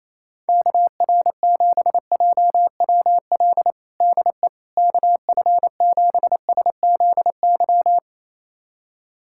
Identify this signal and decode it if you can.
Morse code, heard as a keyed tone, about 28 wpm, 700 Hz. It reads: KR7JWL DE KF7SZY
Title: USA Callsign DE Callsign - 6 Characters - 28wpm